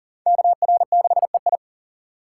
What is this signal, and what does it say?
Morse code, heard as a keyed tone, about 50 wpm, 700 Hz. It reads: KR6EI
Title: Speed Racing - USA Call Signs 40wpm